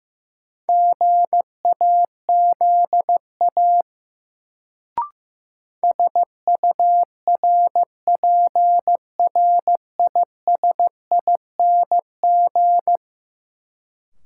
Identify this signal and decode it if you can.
Morse code, heard as a keyed tone, about 15 wpm, 700 Hz. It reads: GAZA SURPRISING
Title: Sets of 5 Words - New Word Drill - 15wpm